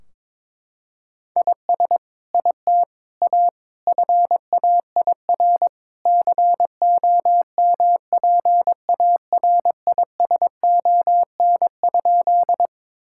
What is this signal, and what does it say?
Morse code, heard as a keyed tone, about 22 wpm, 700 Hz. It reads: IS IT A FAIR COMPARISON?
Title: Sentences from Top 1600 Words 22wpm